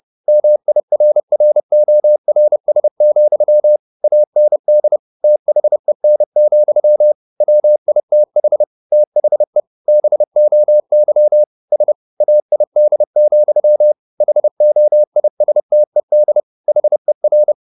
Morse code, heard as a keyed tone, about 30 wpm, 600 Hz. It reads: MIRRORS, AND THEN, WITH THE BOY S AID, HOISTED HER
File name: the old man and the sea / the old man and the sea 1 30wpm